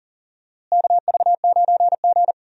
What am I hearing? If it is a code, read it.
Morse code, heard as a keyed tone, about 37 wpm, 700 Hz. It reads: KV9G
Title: USA Call Signs - 4 Characters - Repeated 3x - 40wpm